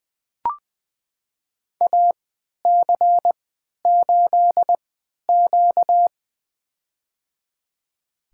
Morse code, heard as a keyed tone, about 20 wpm, spaced as for 11 wpm, 700 Hz. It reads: AC8Q
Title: USA Call Signs - 4 Characters - 3x Character Spacing - 20wpm